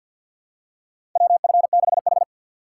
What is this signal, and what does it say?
Morse code, heard as a keyed tone, about 47 wpm, 700 Hz. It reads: WF6H
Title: USA Call Signs - 4 Characters - 2x Character Spacing - 50wpm